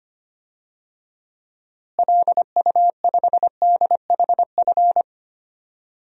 Morse code, heard as a keyed tone, about 25 wpm, 700 Hz. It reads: LU5DHF
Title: DXCC-100 Call Signs - 6 Characters - Repeated 3x - 25wpm